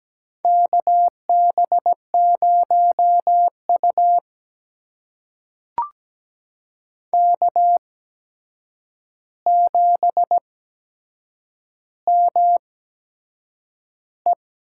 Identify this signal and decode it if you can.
Morse code, heard as a keyed tone, about 17 wpm, spaced as for 7 wpm, 700 Hz. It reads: KB0U K7ME
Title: USA Call Signs - 4 Characters - 8x Character Spacing - 17wpm